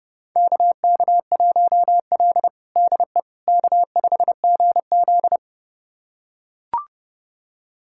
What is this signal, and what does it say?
Morse code, heard as a keyed tone, about 30 wpm, 700 Hz. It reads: KK1L DE K5GZ
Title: USA Callsign DE Callsign - 4 Characters - 30wpm